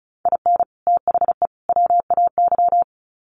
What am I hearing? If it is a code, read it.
Morse code, heard as a keyed tone, about 34 wpm, 700 Hz. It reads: IN THE WAY
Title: Sets of 3 Words 35wpm